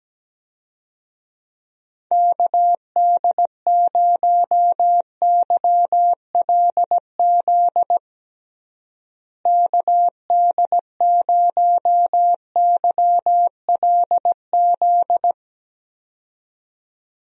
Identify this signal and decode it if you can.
Morse code, heard as a keyed tone, about 17 wpm, 700 Hz. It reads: KD0YLZ KD0YLZ
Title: USA Call Signs - 6 Characters - Repeated 2x - 17wpm